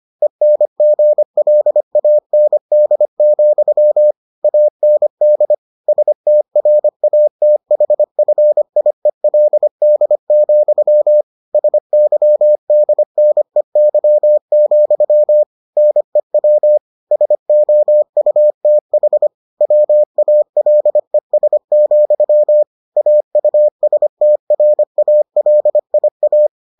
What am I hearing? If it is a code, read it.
Morse code, heard as a keyed tone, about 25 wpm, 600 Hz. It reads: ENGLAND, AND STRATHFIELD, SYDNEY, NEW SOUTH WALES, AUSTRALIA